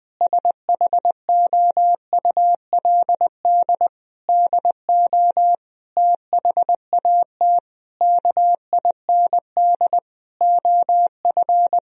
Morse code, heard as a keyed tone, about 20 wpm, 700 Hz. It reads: SHOULD DO THAT KIND OF
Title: Sentences from Top 1000 Words - Review Part 2 - 20wpm